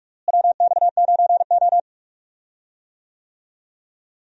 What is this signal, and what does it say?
Morse code, heard as a keyed tone, about 47 wpm, 700 Hz. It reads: WX9O
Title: USA Call Signs - 4 Characters - Repeated 2x - 45wpm